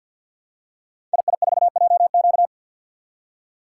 Morse code, heard as a keyed tone, about 48 wpm, 700 Hz. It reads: II4JQ
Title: Code Groups - 5 Letter:Numbers - 50wpm